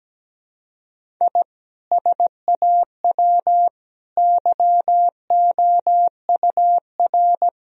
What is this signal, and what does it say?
Morse code, heard as a keyed tone, about 17 wpm, 700 Hz. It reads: I SAW YOUR
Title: Sentences from Top 1000 Words - Review Part 1 - 17wpm